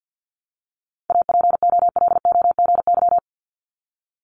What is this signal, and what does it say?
Morse code, heard as a keyed tone, about 50 wpm, 700 Hz. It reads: APOLOGY
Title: Multi-letter Phonemes 50wpm